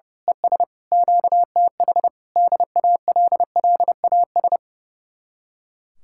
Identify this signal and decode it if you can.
Morse code, heard as a keyed tone, about 28 wpm, 700 Hz. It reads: ES QTH DALLAS
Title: Speed-Racing Common QSO Phrases - 30wpm